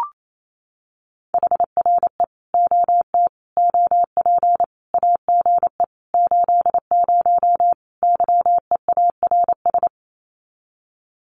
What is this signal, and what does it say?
Morse code, heard as a keyed tone, about 28 wpm, 700 Hz. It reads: HRE OT OP AGE 80 YEARS
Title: Common QSO Phrases 28wpm